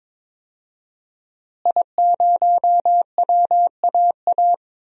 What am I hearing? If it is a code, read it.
Morse code, heard as a keyed tone, about 21 wpm, 700 Hz. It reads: I0WAA
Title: DXCC-100 Call Signs - 5 Characters - Repeated 2x - 22wpm